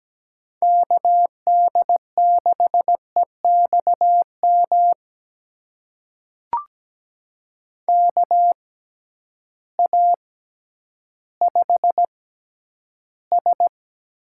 Morse code, heard as a keyed tone, about 17 wpm, spaced as for 10 wpm, 700 Hz. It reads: KD6EXM KA5S
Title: USA Call Signs - 6 Characters - 6x Character Spacing - 17wpm